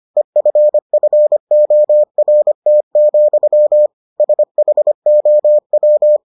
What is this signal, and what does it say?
Morse code, heard as a keyed tone, about 25 wpm, 600 Hz. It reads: EFFORT, SHOW